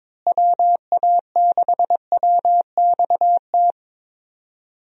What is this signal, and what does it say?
Morse code, heard as a keyed tone, about 22 wpm, 700 Hz. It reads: WA6WXT
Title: USA Call Signs - 6 Characters - Repeated 2x - 22wpm